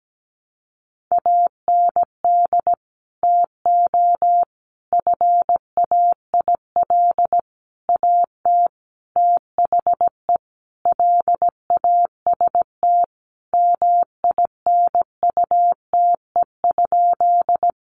Morse code, heard as a keyed tone, about 17 wpm, 700 Hz. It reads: AND TO FAIL AT THE LAST MINUTE?
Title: Sentences from Top 1100 Words 17wpm